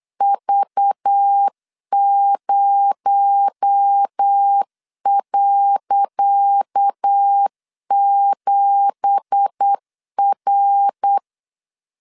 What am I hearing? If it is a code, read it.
Morse code, heard as a keyed tone, about 8 wpm, 800 Hz. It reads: V0.7R
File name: LEZIONI / 10 SIGNS / 10C DOT 8wpm 3649